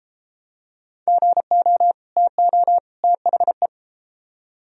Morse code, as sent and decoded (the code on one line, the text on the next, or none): --. --- / - --- / - .... .
GO TO THE